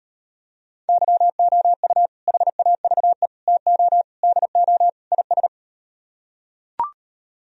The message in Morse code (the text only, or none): YOU HAVE TO DO IS